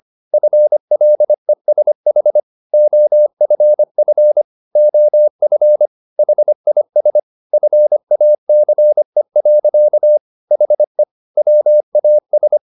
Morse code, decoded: FLESH OFF OF HIS FACE. HE WAS

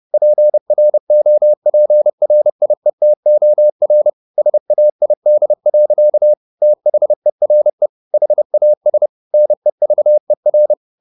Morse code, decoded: PROPRIETOR SAID. THERE HAS NEVER

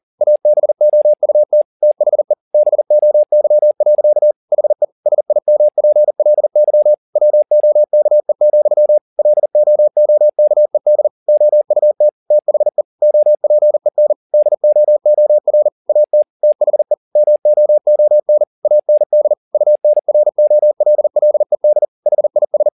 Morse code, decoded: ABOUT THE BOY. HE SIMPLY WOKE, LOOKED OUT THE OPEN DOOR AT THE MOON AND UNROLLED HIS

